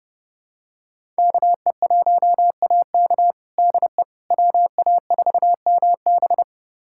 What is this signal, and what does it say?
Morse code, heard as a keyed tone, about 30 wpm, 700 Hz. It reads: KE1AK DE WA4MB
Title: USA Callsign DE Callsign - 5 Characters - 30wpm